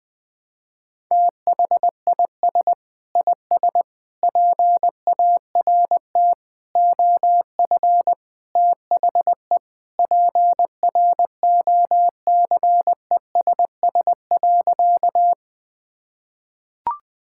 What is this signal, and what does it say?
Morse code, heard as a keyed tone, about 20 wpm, 700 Hz. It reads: THIS IS PART OF THE PROCESS.
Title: Sentences with 2-Word Phrases - 20wpm